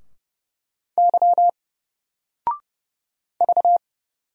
Morse code, heard as a keyed tone, about 30 wpm, 700 Hz. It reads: Y V
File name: Single-Letter-Number-30wpm